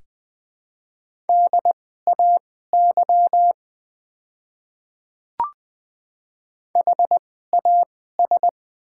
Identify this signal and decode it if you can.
Morse code, heard as a keyed tone, about 20 wpm, spaced as for 15 wpm, 700 Hz. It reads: DAY HAS
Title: 2x Character Spacing - Top 100 Words - 20wpm